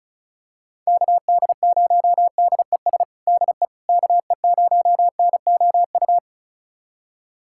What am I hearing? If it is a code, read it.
Morse code, heard as a keyed tone, about 36 wpm, 690 Hz. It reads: KD0DES DE KE0NOU
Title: USA Callsign DE Callsign - 6 Characters - 35wpm